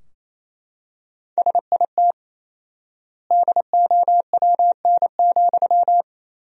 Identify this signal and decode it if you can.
Morse code, heard as a keyed tone, about 28 wpm, 700 Hz. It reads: SIT DOWN,
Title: 4x Word-Spacing - Sentences from Top 500 Words - 28wpm